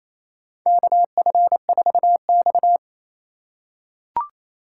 Morse code, heard as a keyed tone, about 28 wpm, 700 Hz. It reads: KF4X
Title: USA Call Signs - 4 Characters - 3x Character Spacing - 28wpm